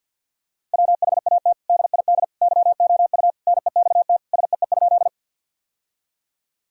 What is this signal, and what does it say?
Morse code, heard as a keyed tone, about 52 wpm, 700 Hz. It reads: WHAT DID YOU NEXT SEE?
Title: Sentences from Top 1000 Words - Review Part 1 - 50wpm